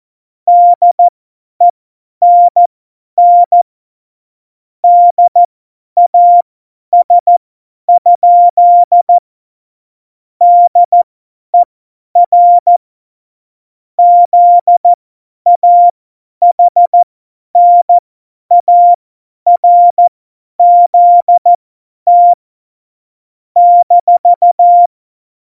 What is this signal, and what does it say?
Morse code, heard as a keyed tone, about 14 wpm, spaced as for 10 wpm, 700 Hz. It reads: DENN DAS? DER ZAHNARZT -
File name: Morsekurs / Witze / Witz 08. September 2020, 50 F70